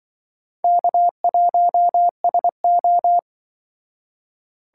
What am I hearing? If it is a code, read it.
Morse code, heard as a keyed tone, about 24 wpm, 700 Hz. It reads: K1SO
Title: Speed Racing - USA 1x2 Call Signs 20wpm